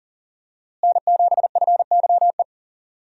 Morse code, heard as a keyed tone, about 40 wpm, 700 Hz. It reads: N7FYE
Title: USA Call Signs - 5 Characters - Repeated 3x - 40wpm